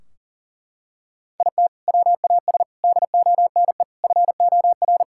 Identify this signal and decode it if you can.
Morse code, heard as a keyed tone, about 37 wpm, 700 Hz. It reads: IT WAS DONE FOR
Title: Sentences from Top 1600 Words 40wpm